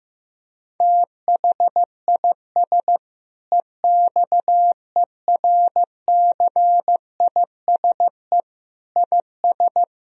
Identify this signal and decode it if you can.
Morse code, heard as a keyed tone, about 15 wpm, 700 Hz. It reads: THIS EXERCISE IS
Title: Sentences from Top 1000 Words - Review Part 4 - 15wpm